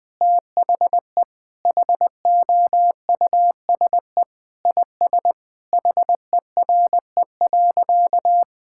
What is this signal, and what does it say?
Morse code, heard as a keyed tone, about 19 wpm, 700 Hz. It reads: THE HOUSE IS HERE.
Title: Sentences with 3-Word Phrases - Spoken Beginning - 20wpm